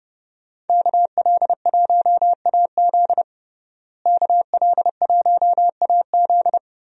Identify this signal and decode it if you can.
Morse code, heard as a keyed tone, about 32 wpm, 700 Hz. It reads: KL1AZ KL1AZ
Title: USA Call Signs - 5 Characters - Repeated 3x - 30wpm